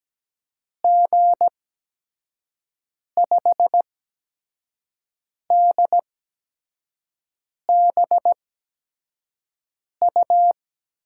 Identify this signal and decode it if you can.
Morse code, heard as a keyed tone, about 17 wpm, spaced as for 5 wpm, 700 Hz. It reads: G5DBU
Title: USA Call Signs - 6 Characters - 8x Character Spacing - 17wpm